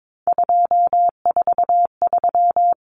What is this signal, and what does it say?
Morse code, heard as a keyed tone, about 22 wpm, 700 Hz. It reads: 243
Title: Numbers - 3-Digit 22wpm